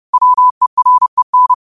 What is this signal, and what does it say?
Morse code, heard as a keyed tone, about 30 wpm, 1000 Hz. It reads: WEREN
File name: qotdfiles / 30wpm